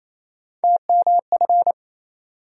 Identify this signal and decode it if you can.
Morse code, heard as a keyed tone, about 28 wpm, 700 Hz. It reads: TMF
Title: USA Call Sign Postfixes - 3 Letters 28wpm